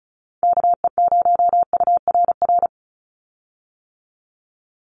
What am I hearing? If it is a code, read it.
Morse code, heard as a keyed tone, about 34 wpm, 700 Hz. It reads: KE0URR
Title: USA Call Signs 35wpm